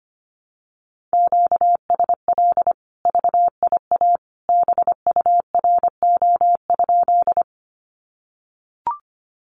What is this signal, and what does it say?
Morse code, heard as a keyed tone, about 25 wpm, 700 Hz. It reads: QSL VIA BURO?
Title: Common QSO Phrases 25wpm